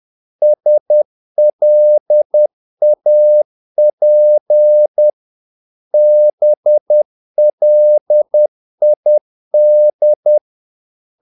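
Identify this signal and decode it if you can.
Morse code, heard as a keyed tone, about 10 wpm, 600 Hz. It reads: SLAP BLID
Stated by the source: Morse practice of short Danish words at 10 wpm, from CW med Gnister.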